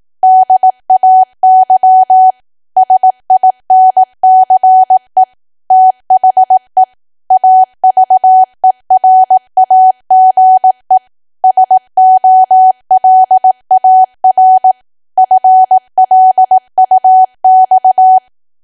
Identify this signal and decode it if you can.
Morse code, heard as a keyed tone, about 18 wpm, 750 Hz. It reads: DAY SINCE THE AVERAGE SOLAR FLUX
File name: cwa / W1AW-Level2 / 18WPM